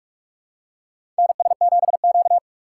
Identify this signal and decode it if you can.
Morse code, heard as a keyed tone, about 43 wpm, 700 Hz. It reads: NS7Q
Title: USA Call Signs - 2x Character Spacing - 45wpm